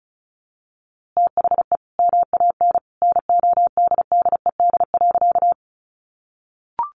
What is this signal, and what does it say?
Morse code, heard as a keyed tone, about 34 wpm, 700 Hz. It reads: THE MAN NODDED.
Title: A Sound Of Thunder 35wpm